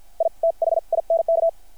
Morse code, heard as a keyed tone, about 50 wpm, spaced as for 43 wpm, 660 Hz. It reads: I THINK
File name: archived / CRSnet